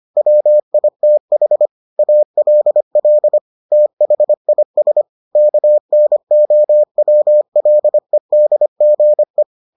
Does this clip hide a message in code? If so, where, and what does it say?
Yes: Morse code from 0.2 to 9.4 s, heard as a keyed tone, about 25 wpm, 600 Hz. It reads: WITH ALL THIS KNOWLEDGE